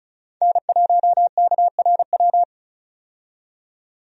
Morse code, heard as a keyed tone, about 36 wpm, 700 Hz. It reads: N1KRW
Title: USA Call Signs - 5 Characters - 35wpm